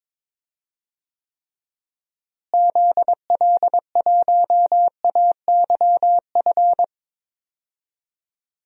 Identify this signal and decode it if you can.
Morse code, heard as a keyed tone, about 22 wpm, 700 Hz. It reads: ZL1AYF
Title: DXCC-100 Call Signs - 6 Characters - Repeated 3x - 22wpm